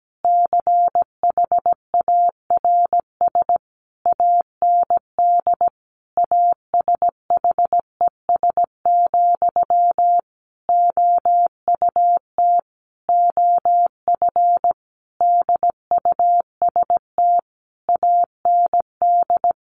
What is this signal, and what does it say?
Morse code, heard as a keyed tone, about 17 wpm, 700 Hz. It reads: CHARS AND ASHES, OUT OF DUST AND
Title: A Sound Of Thunder 17wpm